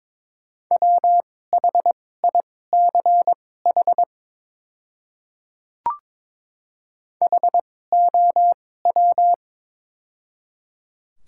Morse code, heard as a keyed tone, about 22 wpm, spaced as for 16 wpm, 690 Hz. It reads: WHICH HOW
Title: 2x Character Spacing - Top 100 Words - 22wpm